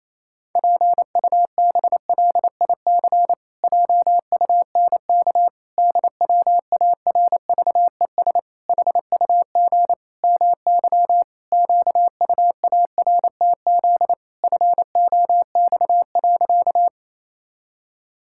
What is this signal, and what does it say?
Morse code, heard as a keyed tone, about 28 wpm, 700 Hz. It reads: PUBLIC JUNK DWARVES HUG MY QUARTZ FOX.
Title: Pangram Challenge v2 - 28wpm